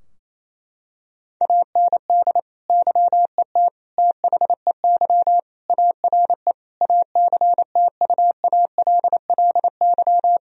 Morse code, heard as a keyed tone, about 28 wpm, 700 Hz. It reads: AND YET THEY ARE ACTUALLY
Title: Sentences from Top 1000 Words - Review Part 4 - 28wpm